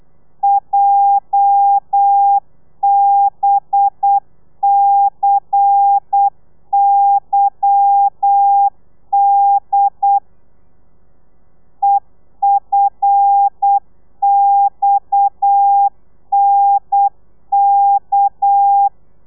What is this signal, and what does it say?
Morse code, heard as a keyed tone, about 8 wpm, 800 Hz. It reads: JBCYD EFXNK
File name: Lez11